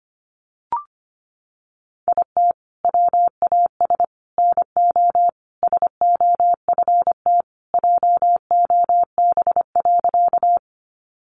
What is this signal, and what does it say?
Morse code, heard as a keyed tone, about 25 wpm, 700 Hz. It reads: IT WAS NO SOFT JOB.
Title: Sentences from Top 1500 Words 25wpm